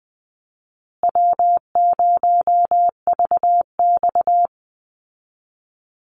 Morse code, heard as a keyed tone, about 20 wpm, 700 Hz. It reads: W0VX
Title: Easy Call Signs Repeated 2x - 20wpm